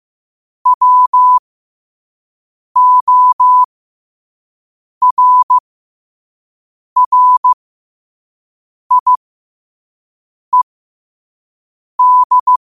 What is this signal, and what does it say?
Morse code, heard as a keyed tone, about 15 wpm, spaced as for 5 wpm, 1000 Hz. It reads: WORRIED
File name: qotdfiles / 5wpm